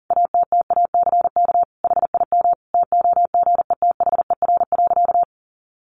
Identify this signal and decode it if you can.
Morse code, heard as a keyed tone, about 40 wpm, 700 Hz. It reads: ATTACK HIM TOGETHER.
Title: Sentences from Top 1500 Words 40wpm